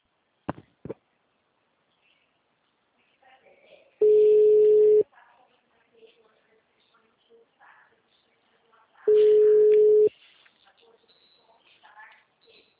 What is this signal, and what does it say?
Ringback tone (Europe, 425 Hz), with speech.
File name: out- -2008- - - .18762